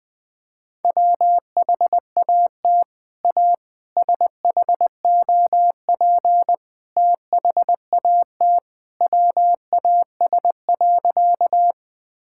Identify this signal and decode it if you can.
Morse code, heard as a keyed tone, about 21 wpm, 700 Hz. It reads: WHAT A SHOP THAT WAS.
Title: Sentences from Top 1600 Words 20wpm